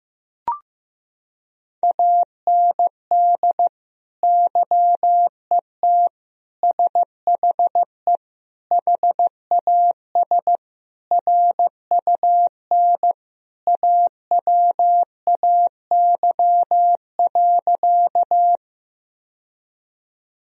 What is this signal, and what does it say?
Morse code, heard as a keyed tone, about 15 wpm, 700 Hz. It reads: AND YET SHE HAS RUN AWAY.
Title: Sentences from Top 1000 Words - Review Part 1 - 15wpm